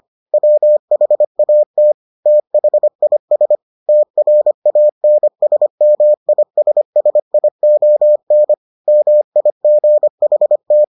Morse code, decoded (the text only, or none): WHAT THIS TRANSMISSION MIGHT